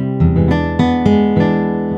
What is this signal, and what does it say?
There is no code.